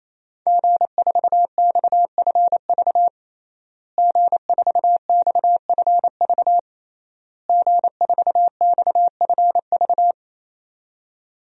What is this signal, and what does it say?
Morse code, heard as a keyed tone, about 28 wpm, 700 Hz. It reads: G4XFV G4XFV G4XFV
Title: DXCC-100 Call Signs - 5 Characters - Repeated 3x - 28wpm